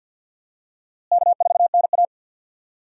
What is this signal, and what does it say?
Morse code, heard as a keyed tone, about 50 wpm, 700 Hz. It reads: K4NA
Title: USA Call Signs - 4 Characters - Repeated 2x - 50wpm